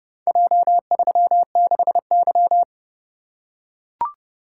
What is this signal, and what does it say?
Morse code, heard as a keyed tone, about 28 wpm, 700 Hz. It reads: J36Y